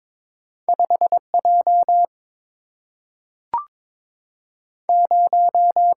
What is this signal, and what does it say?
Morse code, heard as a keyed tone, about 22 wpm, 700 Hz. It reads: 5 J 0
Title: Single-Letter Single-Number 22wpm